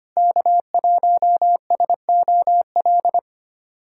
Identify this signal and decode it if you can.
Morse code, heard as a keyed tone, about 25 wpm, 700 Hz. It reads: K1SOL
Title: USA Callsign DE Callsign - 5 Characters - 25wpm